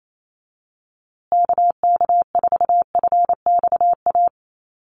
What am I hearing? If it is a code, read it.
Morse code, heard as a keyed tone, about 28 wpm, 700 Hz. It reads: KK4FXA
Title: USA Call Signs Repeated 2x 28wpm